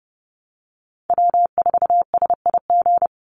Morse code, heard as a keyed tone, about 30 wpm, 700 Hz. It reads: W4SIG
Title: Call Signs Repeated 3x - 30wpm